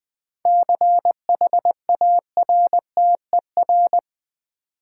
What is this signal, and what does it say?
Morse code, heard as a keyed tone, about 20 wpm, 700 Hz. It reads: CHARTER
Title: 7 Letter Words - First 3 Letters Given - 20wpm